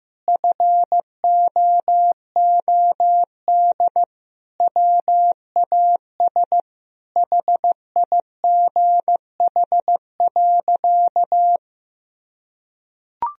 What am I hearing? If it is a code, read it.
Morse code, heard as a keyed tone, about 15 wpm, 700 Hz. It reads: FOOD WAS HIGH.
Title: Sentences from Top 1000 Words - Review Part 4 - 15wpm